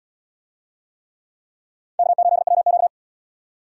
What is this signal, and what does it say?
Morse code, heard as a keyed tone, about 71 wpm, 700 Hz. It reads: K8WQ